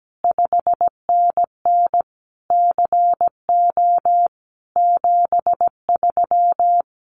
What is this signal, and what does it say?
Morse code, heard as a keyed tone, about 17 wpm, 700 Hz. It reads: 5NN CO 73
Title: Common QSO Phrases Encore 17wpm